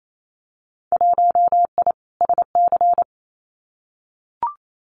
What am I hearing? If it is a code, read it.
Morse code, heard as a keyed tone, about 28 wpm, 700 Hz. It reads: 1I SC
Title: Winter Field Day Exchanges 28wpm